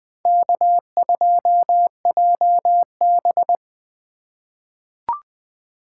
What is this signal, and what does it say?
Morse code, heard as a keyed tone, about 20 wpm, 700 Hz. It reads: K2JB
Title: 2023 SOTA Chaser Call Signs - 20wpm